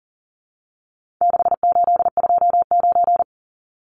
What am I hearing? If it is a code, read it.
Morse code, heard as a keyed tone, about 40 wpm, 700 Hz. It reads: 6829
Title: Numbers - 4-Digit 40wpm